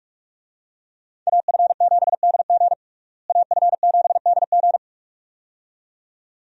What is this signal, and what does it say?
Morse code, heard as a keyed tone, about 46 wpm, 700 Hz. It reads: AF7DG AF7DG
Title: USA Call Signs - 5 Characters - Repeated 2x - 45wpm